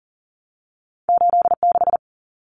Morse code, heard as a keyed tone, about 41 wpm, 700 Hz. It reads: 86
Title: Numbers - 2-Digit 40wpm